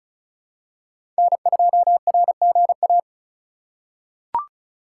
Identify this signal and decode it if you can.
Morse code, heard as a keyed tone, about 36 wpm, 700 Hz. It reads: N2RGA
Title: USA Call Signs - 5 Characters - 2x Character Spacing - 35wpm